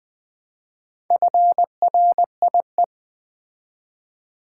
Morse code, heard as a keyed tone, about 20 wpm, 700 Hz. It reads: FRIE
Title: Build A Word - 2 Letters At A Time - 8-Letter Words - 20wpm